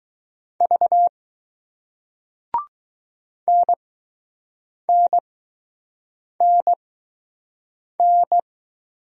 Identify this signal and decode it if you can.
Morse code, heard as a keyed tone, about 20 wpm, spaced as for 6 wpm, 700 Hz. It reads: V NNNN